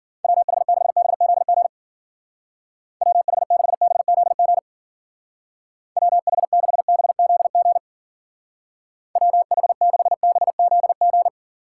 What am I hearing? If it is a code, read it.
Morse code, heard as a keyed tone, about 50 wpm, 700 Hz. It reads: WH6BZG WH6BZG WH6BZG WH6BZG
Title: Speed Racing - USA Call Signs 40wpm